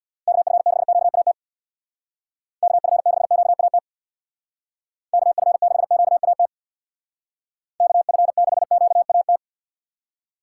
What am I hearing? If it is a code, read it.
Morse code, heard as a keyed tone, about 65 wpm, 700 Hz. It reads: KF6QAT KF6QAT KF6QAT KF6QAT